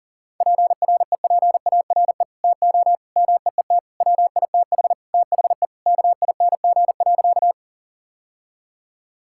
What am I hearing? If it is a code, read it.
Morse code, heard as a keyed tone, about 40 wpm, 700 Hz. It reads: PREPARE TO MEET WITH THE KING.
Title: Sentences from Top 1000 Words - Review Part 6 - 40wpm